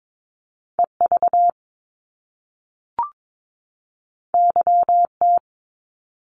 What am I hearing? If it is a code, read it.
Morse code, heard as a keyed tone, about 22 wpm, 700 Hz. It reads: EV YT
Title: Top Two Letter Combinations in Words 22wpm